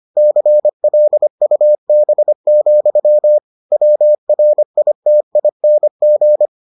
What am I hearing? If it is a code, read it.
Morse code, heard as a keyed tone, about 25 wpm, 600 Hz. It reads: CLUB, WRITING